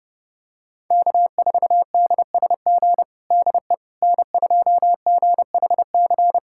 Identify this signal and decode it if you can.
Morse code, heard as a keyed tone, about 29 wpm, 700 Hz. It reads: K4DSG DE N2GHC
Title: USA Callsign DE Callsign - 5 Characters - 30wpm